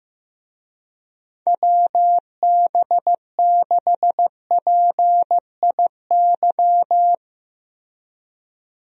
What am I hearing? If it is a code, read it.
Morse code, heard as a keyed tone, about 15 wpm, 700 Hz. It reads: WB6PIY
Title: USA Call Signs - 6 Characters - Repeated 2x - 15wpm